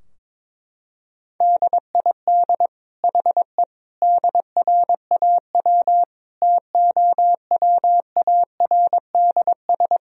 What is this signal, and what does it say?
Morse code, heard as a keyed tone, about 22 wpm, 700 Hz. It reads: DID HE DRAW TOWARDS
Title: Sentences from Top 1000 Words - Review Part 2 - 22wpm